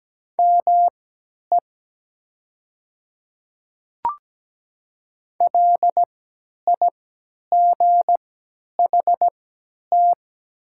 Morse code, heard as a keyed tone, about 17 wpm, spaced as for 10 wpm, 700 Hz. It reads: ME LIGHT
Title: 3x Character Spacing - Top 200 Words - 17wpm